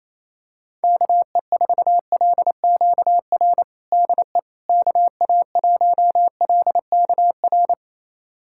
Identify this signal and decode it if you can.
Morse code, heard as a keyed tone, about 28 wpm, 700 Hz. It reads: KE4LQR DE KA1LKR
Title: USA Callsign DE Callsign - 6 Characters - 28wpm